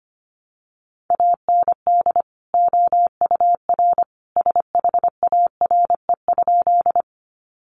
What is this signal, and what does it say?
Morse code, heard as a keyed tone, about 25 wpm, 700 Hz. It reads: AND OUR SHARE?
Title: Sentences from Top 900 Words 25wpm